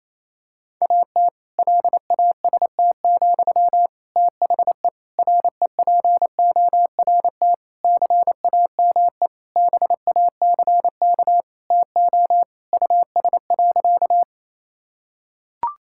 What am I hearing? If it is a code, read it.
Morse code, heard as a keyed tone, about 28 wpm, 700 Hz. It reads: AT LAST, THE REPORT CAME BACK TO US.
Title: Sentences from Top 1000 Words - Review Part 6 - 28wpm